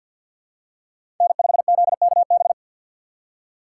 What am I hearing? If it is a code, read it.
Morse code, heard as a keyed tone, about 53 wpm, 700 Hz. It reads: N5ZKB
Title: USA Call Signs - 5 Characters - 2x Character Spacing - 50wpm